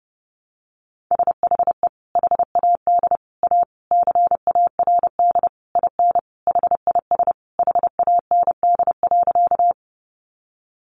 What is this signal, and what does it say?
Morse code, heard as a keyed tone, about 30 wpm, 700 Hz. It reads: SHE HAD A CARD IN HIS HAND.
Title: Sentences from Top 600 Words 30wpm